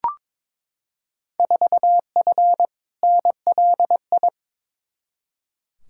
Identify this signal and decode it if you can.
Morse code, heard as a keyed tone, about 21 wpm, 700 Hz. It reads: 4F NLI
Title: Field Day Exchanges Encore - 22wpm